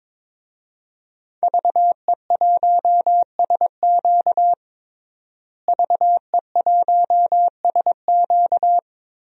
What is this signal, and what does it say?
Morse code, heard as a keyed tone, about 22 wpm, 700 Hz. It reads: VE1SQ VE1SQ
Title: DXCC-100 Call Signs - 5 Characters - Repeated 2x - 22wpm